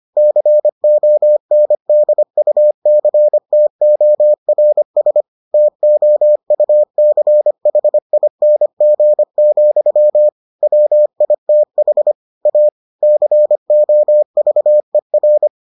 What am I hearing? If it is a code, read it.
Morse code, heard as a keyed tone, about 25 wpm, 600 Hz. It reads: CONDUCTORS TOUCHING, WITH A COVER